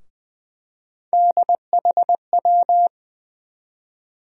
Morse code, heard as a keyed tone, about 21 wpm, 700 Hz. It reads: DHW
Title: USA Call Sign Postfixes - 3 Letters 20wpm